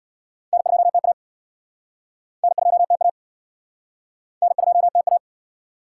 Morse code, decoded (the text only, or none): N2TA N2TA N2TA